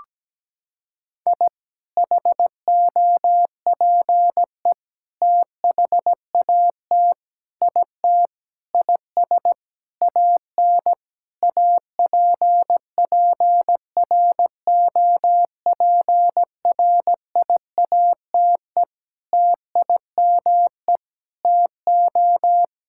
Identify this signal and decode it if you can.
Morse code, heard as a keyed tone, about 17 wpm, 700 Hz. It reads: I HOPE THAT IT IS AN APPROPRIATE TIME TO
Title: Sentences from Top 1000 Words - Review Part 5 - 17wpm